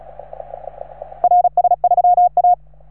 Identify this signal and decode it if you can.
Morse code, heard as a keyed tone, about 35 wpm, 710 Hz. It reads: RS3A
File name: RS3A-20